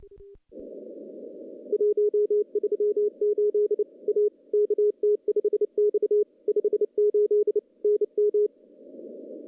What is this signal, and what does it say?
Morse code, heard as a keyed tone, about 29 wpm, 420 Hz. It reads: 138 A KT5X 58 NM
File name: contests / CW-SS-k5tr- -1400